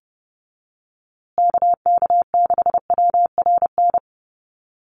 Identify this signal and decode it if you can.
Morse code, heard as a keyed tone, about 30 wpm, 700 Hz. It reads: KK6WRN
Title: USA Call Signs Repeated 2x 30wpm